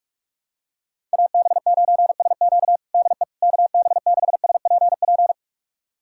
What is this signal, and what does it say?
Morse code, heard as a keyed tone, about 41 wpm, 690 Hz. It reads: AB9SQ DE KB6SPP
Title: USA Callsign DE Callsign 45wpm